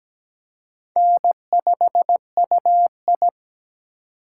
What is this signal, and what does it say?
Morse code, heard as a keyed tone, about 17 wpm, 700 Hz. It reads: N5UI
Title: K1USN SST QSO Practice - 17wpm